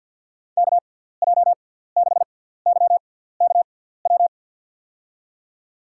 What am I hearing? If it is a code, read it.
Morse code, heard as a keyed tone, about 46 wpm, spaced as for 17 wpm, 700 Hz. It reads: KJ6YKW
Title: USA Call Signs - 6 Characters - 6x Character Spacing - 50wpm